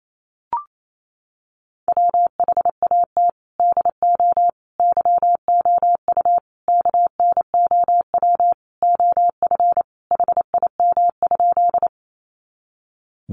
Morse code, heard as a keyed tone, about 28 wpm, 700 Hz. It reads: WHAT DO YOU KNOW OF HIM?
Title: Sentences from Top 100 Words 28wpm